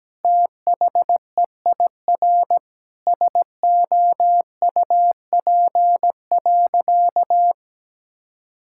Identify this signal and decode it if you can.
Morse code, heard as a keyed tone, about 17 wpm, 700 Hz. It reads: THEIR SOUP.
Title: Sentences from Top 1000 Words - Review Part 6 - 17wpm